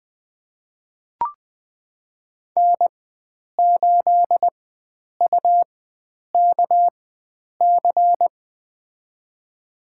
Morse code, heard as a keyed tone, about 20 wpm, spaced as for 9 wpm, 700 Hz. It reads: N8UKC
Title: USA Call Signs - 4x Character Spacing - 20wpm